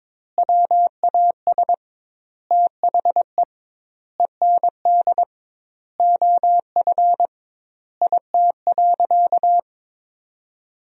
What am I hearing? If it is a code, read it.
Morse code, heard as a keyed tone, about 22 wpm, 700 Hz. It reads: WAS THE END OF IT.
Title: 2x Word-Spacing - Sentences from Top 200 Words - 22wpm